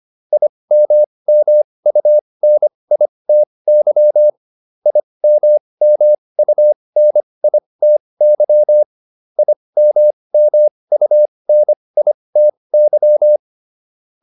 Morse code, heard as a keyed tone, about 25 wpm, spaced as for 20 wpm, 600 Hz. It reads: IMMUNITY IMMUNITY IMMUNITY
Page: /advanced-practice-files/